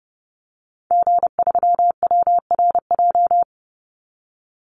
Morse code, heard as a keyed tone, about 30 wpm, 700 Hz. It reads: G3WRJ
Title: Call Signs - 30wpm